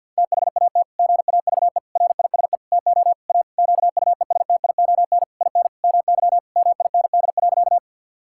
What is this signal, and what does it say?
Morse code, heard as a keyed tone, about 51 wpm, 700 Hz. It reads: THAT GAVE RISE TO A QUESTION IN MY MIND.
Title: Sentences from Top 1000 Words - Review Part 6 - 50wpm